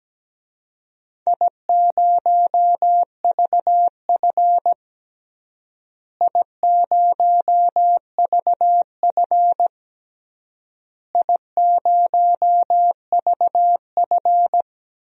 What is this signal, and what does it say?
Morse code, heard as a keyed tone, about 17 wpm, 700 Hz. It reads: I0VF I0VF I0VF
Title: DXCC-100 Call Signs - 4 Characters - Repeated 3x - 17wpm